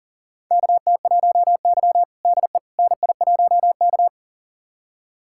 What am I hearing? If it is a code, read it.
Morse code, heard as a keyed tone, about 37 wpm, 700 Hz. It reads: KT1Y DE NI1K
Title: USA Callsign DE Callsign - 4 Characters - 40wpm